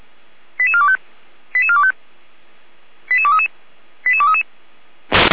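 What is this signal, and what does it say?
Five-tone selective call (ZVEI), heard as quick runs of five tones, tones of about 70 ms. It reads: 88425 88425 88233 88233
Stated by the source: ZVEI five-tone selective call received on an SDR.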